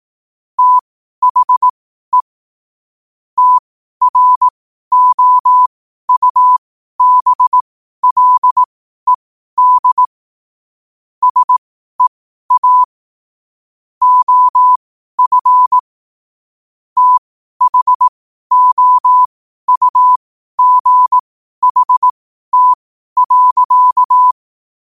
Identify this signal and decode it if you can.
Morse code, heard as a keyed tone, about 18 wpm, spaced as for 12 wpm, 1000 Hz. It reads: THE TROUBLED SEA OF THOUGHT.
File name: qotdfiles / 12wpm